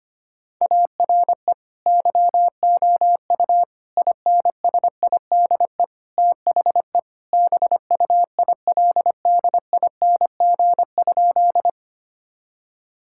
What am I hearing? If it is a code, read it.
Morse code, heard as a keyed tone, about 25 wpm, 700 Hz. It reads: ARE YOU INSIDE THE BUILDING?
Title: Sentences from Top 1000 Words - Review Part 6 - 25wpm